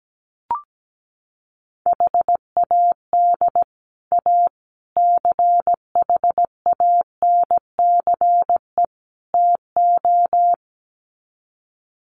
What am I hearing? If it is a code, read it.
Morse code, heard as a keyed tone, about 17 wpm, 700 Hz. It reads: HAD A CHANCE TO
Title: Sets of 4 Words 17wpm